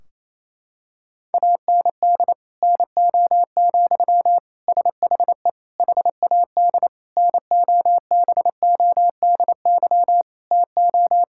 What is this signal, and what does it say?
Morse code, heard as a keyed tone, about 28 wpm, 700 Hz. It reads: AND NO, SHE HAD NOBODY TO
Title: Sentences from Top 1600 Words 28wpm